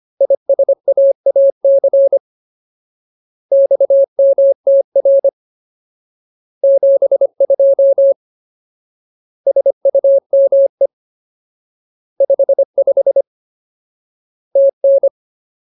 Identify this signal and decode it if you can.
Morse code, heard as a keyed tone, about 25 wpm, 550 Hz. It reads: ISAAC XMTR 72 SUME 55 TN
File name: Ultimate Word List / 25WPM Ultimate